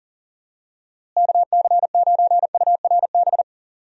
Morse code, heard as a keyed tone, about 38 wpm, 700 Hz. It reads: KC9URB
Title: USA Call Signs - 6 Characters - Repeated 2x - 40wpm